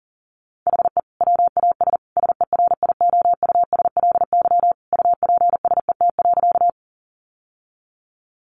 Morse code, heard as a keyed tone, about 37 wpm, 700 Hz. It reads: HE WAS SERIOUSLY UPSET.
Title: Sentences from Top 1200 Words 40wpm